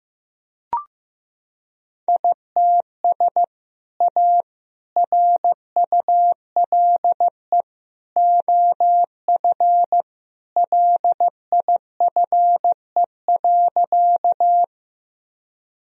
Morse code, heard as a keyed tone, about 15 wpm, 700 Hz. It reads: ITS A RULE OF LIFE.
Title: Sentences from Top 1000 Words - Review Part 2 - 15wpm